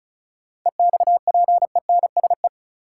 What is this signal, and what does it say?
Morse code, heard as a keyed tone, about 36 wpm, 700 Hz. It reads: EXPENSE
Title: Sets of 4 Words - New Word Drill - 35wpm